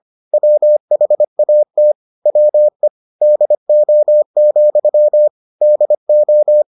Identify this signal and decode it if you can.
Morse code, heard as a keyed tone, about 25 wpm, 600 Hz. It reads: WHAT WE DO, DO